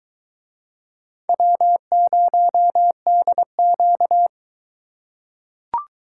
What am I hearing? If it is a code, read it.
Morse code, heard as a keyed tone, about 21 wpm, 700 Hz. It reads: W0DQ